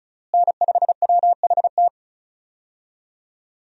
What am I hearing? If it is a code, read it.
Morse code, heard as a keyed tone, about 36 wpm, 700 Hz. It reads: N5WHT
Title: USA Call Signs - 5 Characters - Repeated 2x - 35wpm